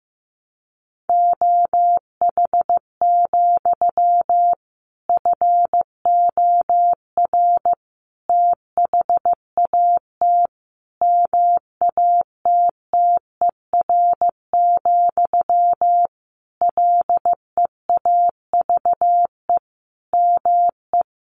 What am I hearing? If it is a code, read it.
Morse code, heard as a keyed tone, about 15 wpm, 700 Hz. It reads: OH, FOR THAT MATTER, LEAVE ME
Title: Sentences from Top 1400 Words 15wpm